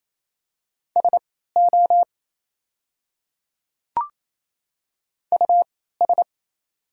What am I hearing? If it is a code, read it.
Morse code, heard as a keyed tone, about 28 wpm, spaced as for 16 wpm, 700 Hz. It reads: SO US
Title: 3x Character Spacing - Top 500 Words - 28wpm